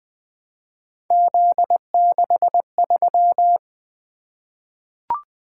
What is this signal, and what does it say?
Morse code, heard as a keyed tone, about 20 wpm, 700 Hz. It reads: Z63
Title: Three Letter-Number Combinations - 20wpm